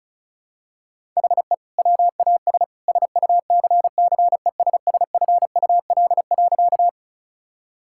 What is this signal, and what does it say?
Morse code, heard as a keyed tone, about 34 wpm, 700 Hz. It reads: HE WAS SUCCESSFUL.
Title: Sentences from Top 1000 Words - Review Part 6 - 35wpm